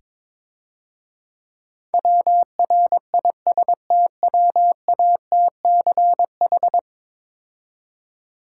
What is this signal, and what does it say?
Morse code, heard as a keyed tone, about 22 wpm, 690 Hz. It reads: WRISTWATCH
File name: Easy Words - Speed-Racing with Hints - 22wpm